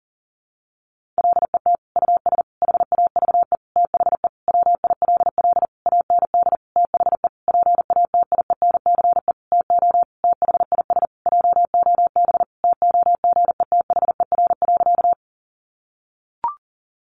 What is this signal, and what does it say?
Morse code, heard as a keyed tone, about 40 wpm, 700 Hz. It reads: LET US HAVE THE WILL AND THE PATIENCE TO THIS JOB TOGETHER.
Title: Sentences from Top 1300 Words 40wpm